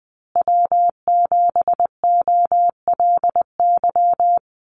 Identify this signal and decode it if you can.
Morse code, heard as a keyed tone, about 20 wpm, 700 Hz. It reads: W7OLY
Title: Call Signs Repeated 3x - 20wpm